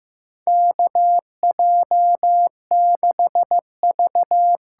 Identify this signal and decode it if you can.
Morse code, heard as a keyed tone, about 15 wpm, 700 Hz. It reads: KJ6V
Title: USA Call Signs - 6 Characters - 8x Character Spacing - 15wpm